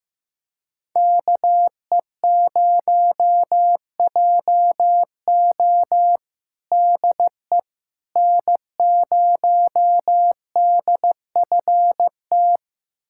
Morse code, heard as a keyed tone, about 15 wpm, 700 Hz. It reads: KE0JO DE N0DFT
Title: USA Callsign DE Callsign - 5 Characters - 15wpm